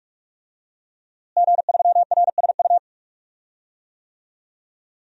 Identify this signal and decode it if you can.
Morse code, heard as a keyed tone, about 45 wpm, 700 Hz. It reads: G3RSU
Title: DXCC-100 Call Signs - 5 Characters - Repeated 2x - 45wpm